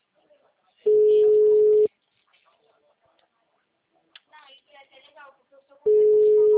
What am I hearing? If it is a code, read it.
Ringback tone (Europe, 425 Hz), with speech.